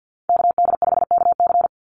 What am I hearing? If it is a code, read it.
Morse code, heard as a keyed tone, about 48 wpm, 700 Hz. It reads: KD5KC
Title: NA SOTA Chaser Callsigns 50wpm